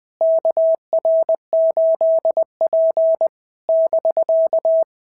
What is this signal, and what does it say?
Morse code, heard as a keyed tone, about 20 wpm, 650 Hz. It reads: KR8P <BK>
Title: POTA QSO Practice - 20wpm